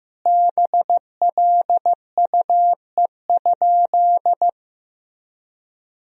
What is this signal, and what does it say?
Morse code, heard as a keyed tone, about 15 wpm, 700 Hz. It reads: BLUE?
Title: Sentences from Top 1000 Words - Review Part 3 - 15wpm